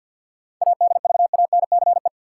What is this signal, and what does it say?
Morse code, heard as a keyed tone, about 51 wpm, 700 Hz. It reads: ADVANCE
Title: Sentences from Top 1400 Words - New Word Drill Alt - 50wpm